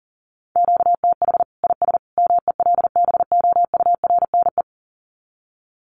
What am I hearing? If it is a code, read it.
Morse code, heard as a keyed tone, about 40 wpm, 700 Hz. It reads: QTH IS MELBOURNE
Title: Common QSO Phrases 40wpm